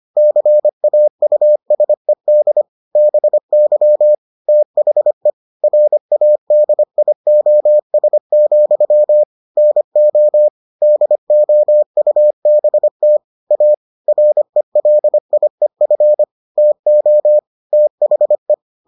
Morse code, heard as a keyed tone, about 25 wpm, 600 Hz. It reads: CAUSED BY THE RADIOS, NO DOUBT A RELIEF TO THE